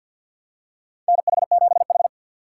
Morse code, heard as a keyed tone, about 50 wpm, 700 Hz. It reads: NH7H
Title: USA Call Signs - 6x Character Spacing - 50wpm